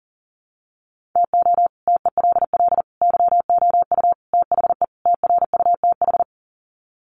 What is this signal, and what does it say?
Morse code, heard as a keyed tone, about 40 wpm, 700 Hz. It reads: TO TELL YOU THE TRUTH
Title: Sets of 5 Words 40wpm